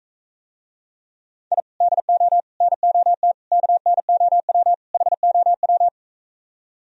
Morse code, heard as a keyed tone, about 45 wpm, 700 Hz. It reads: I DO NOT KNOW HOW